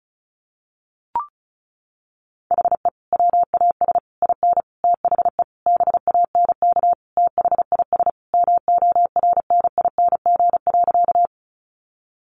Morse code, heard as a keyed tone, about 34 wpm, 700 Hz. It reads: HE WAS IN THE BANK THIS MORNING.
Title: Sentences from Top 900 Words 35wpm